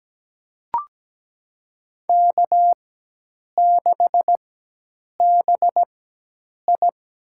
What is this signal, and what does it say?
Morse code, heard as a keyed tone, about 17 wpm, spaced as for 8 wpm, 700 Hz. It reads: K6BI
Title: USA Call Signs - 4 Characters - 4x Character Spacing - 17wpm